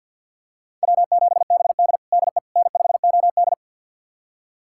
Morse code, heard as a keyed tone, about 46 wpm, 700 Hz. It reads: W7BD DE N5OD
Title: USA Callsign DE Callsign - 4 Characters - 50wpm